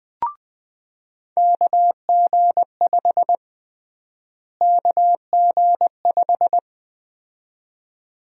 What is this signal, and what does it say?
Morse code, heard as a keyed tone, about 21 wpm, 700 Hz. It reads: KG5 KG5
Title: USA Call Sign Prefixes - Repeated 2x - 20wpm